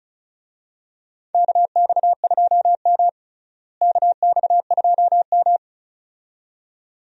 Morse code, heard as a keyed tone, about 36 wpm, 700 Hz. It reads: KX2M KX2M
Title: USA Call Signs - 4 Characters - Repeated 2x - 35wpm